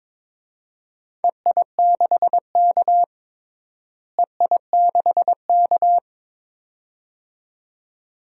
Morse code, heard as a keyed tone, about 22 wpm, 700 Hz. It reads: EI6K EI6K
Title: DXCC-100 Call Signs - 4 Characters - Repeated 2x - 22wpm